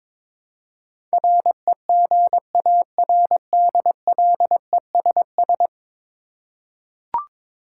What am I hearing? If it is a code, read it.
Morse code, heard as a keyed tone, about 22 wpm, 700 Hz. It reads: REGARDLESS
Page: morse-code-ninja-practice-22wpm